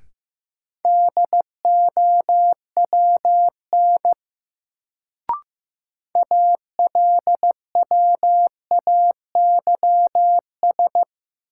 Morse code, heard as a keyed tone, about 15 wpm, 700 Hz. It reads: DOWN ALWAYS
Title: Sentences from Top 300 Words - New Word Drill - 15wpm